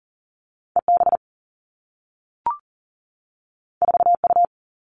Morse code, heard as a keyed tone, about 40 wpm, 700 Hz. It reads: EB 4U
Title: Two Challenging Letter-Number Combinations 40wpm